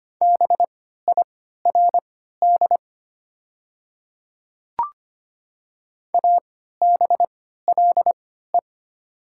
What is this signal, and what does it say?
Morse code, heard as a keyed tone, about 25 wpm, spaced as for 14 wpm, 700 Hz. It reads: BIRD ABLE
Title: 3x Character Spacing - Top 500 Words - 25wpm